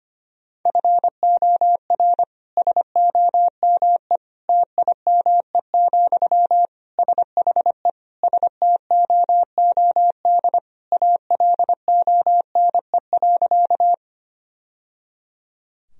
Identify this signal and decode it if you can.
Morse code, heard as a keyed tone, about 25 wpm, 700 Hz. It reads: FOR SOME TIME, SHE STOOD ALONE.
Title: Sentences from Top 1000 Words - Review Part 2 - 25wpm